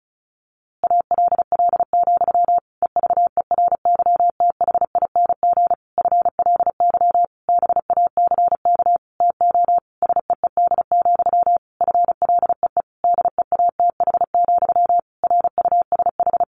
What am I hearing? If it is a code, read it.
Morse code, heard as a keyed tone, about 34 wpm, 700 Hz. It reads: ALL, EVERYTHING FLY BACK TO SEED, FLEE DEATH, RUSH
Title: A Sound Of Thunder 35wpm